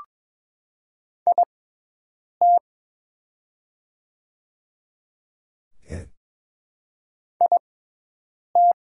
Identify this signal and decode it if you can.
Morse code, heard as a keyed tone, about 22 wpm, spaced as for 8 wpm, 700 Hz. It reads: IT IT
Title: 6x Character Spacing - Top 200 Words - 22wpm